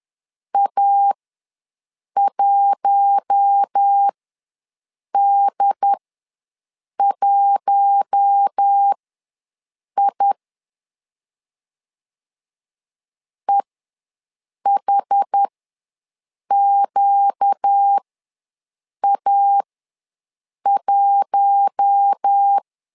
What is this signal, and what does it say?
Morse code, heard as a keyed tone, about 10 wpm, spaced as for 6 wpm, 800 Hz. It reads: A1D1I EHQA1
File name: LEZIONI / 4 Q1 ws 10wpm 2538